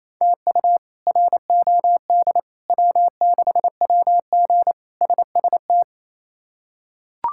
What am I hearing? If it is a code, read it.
Morse code, heard as a keyed tone, about 28 wpm, 700 Hz. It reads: TU ROD W6WG SST
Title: K1USN SST QSO Practice - 28wpm